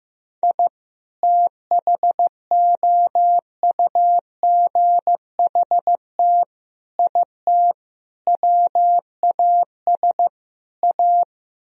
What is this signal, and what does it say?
Morse code, heard as a keyed tone, about 15 wpm, 700 Hz. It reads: I THOUGHT IT WAS A
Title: Speed-Racing - Sets of 5 Words 15wpm